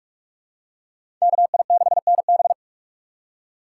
Morse code, heard as a keyed tone, about 47 wpm, 700 Hz. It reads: KI6NB
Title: USA Call Signs - 3x Character Spacing - 45wpm